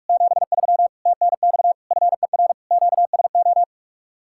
Morse code, heard as a keyed tone, about 49 wpm, 700 Hz. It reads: 73 TNX FER QSO